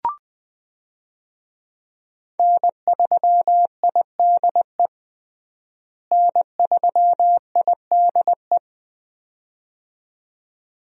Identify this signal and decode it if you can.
Morse code, heard as a keyed tone, about 20 wpm, 700 Hz. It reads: N3IDE N3IDE
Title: USA Call Signs - 5 Characters - Repeated 2x - 20wpm